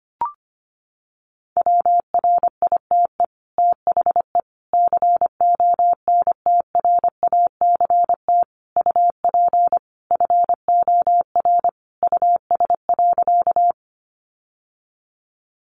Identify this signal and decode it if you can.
Morse code, heard as a keyed tone, about 25 wpm, 700 Hz. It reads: WRITE THE CONTRACT UP FOR US.
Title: Sentences from Top 1400 Words 25wpm